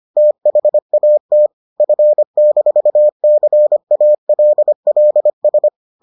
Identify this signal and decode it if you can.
Morse code, heard as a keyed tone, about 25 wpm, 600 Hz. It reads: THAT F-CALLS